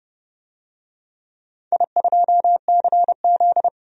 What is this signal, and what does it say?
Morse code, heard as a keyed tone, about 30 wpm, 700 Hz. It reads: I2CZ
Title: DXCC-100 Call Signs - 4 Characters - 30wpm